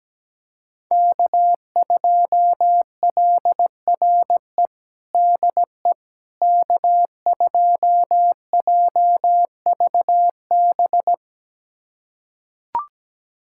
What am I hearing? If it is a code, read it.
Morse code, heard as a keyed tone, about 17 wpm, 700 Hz. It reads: K2LRE DE K2JVB
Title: USA Callsign DE Callsign - 5 Characters - 17wpm